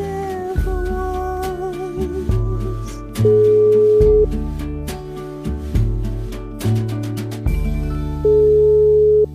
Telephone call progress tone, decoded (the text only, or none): ringback tone